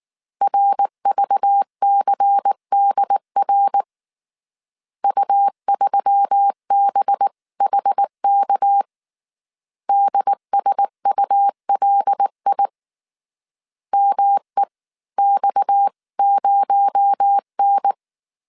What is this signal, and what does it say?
Morse code, heard as a keyed tone, about 19 wpm, 800 Hz. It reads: RVCDR U3BHK DSULI MEX0N